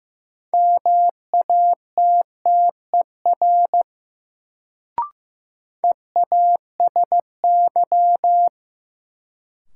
Morse code, heard as a keyed tone, about 15 wpm, 700 Hz. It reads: MATTER EASY
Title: Sentences from Top 700 Words - New Word Drill - 15wpm